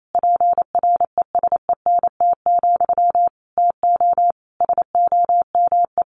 Morse code, heard as a keyed tone, about 28 wpm, 700 Hz. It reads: PRESENT, TO SOME
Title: Sentences from Top 1500 Words 28wpm